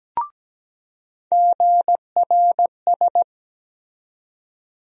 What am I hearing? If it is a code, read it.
Morse code, heard as a keyed tone, about 17 wpm, 700 Hz. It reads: GRS
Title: USA Call Sign Postfixes - 3 Letters 17wpm